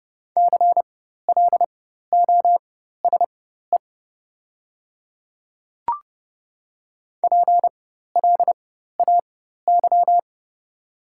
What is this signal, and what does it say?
Morse code, heard as a keyed tone, about 30 wpm, spaced as for 14 wpm, 700 Hz. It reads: CLOSE PLAY
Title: 4x Character Spacing - Top 300 Words - 30wpm